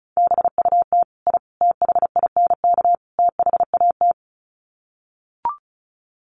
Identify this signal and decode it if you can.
Morse code, heard as a keyed tone, about 34 wpm, 700 Hz. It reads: BUT I THINK THAT
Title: Sets of 4 Words 35wpm